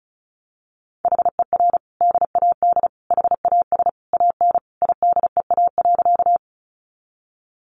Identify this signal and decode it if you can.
Morse code, heard as a keyed tone, about 34 wpm, 700 Hz. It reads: HER DAD HAS AN IDEA.
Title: Additional Easy Phrases 35wpm